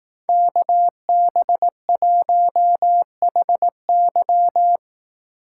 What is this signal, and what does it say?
Morse code, heard as a keyed tone, about 18 wpm, 700 Hz. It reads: KB1HY
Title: Speed Racing - USA 2x2 Call Signs 15wpm